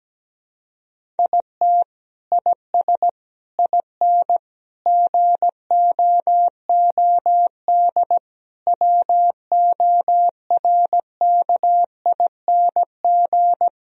Morse code, heard as a keyed tone, about 17 wpm, 700 Hz. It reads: IT IS IN GOOD WORKING
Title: Sentences from Top 1000 Words - Review Part 4 - 17wpm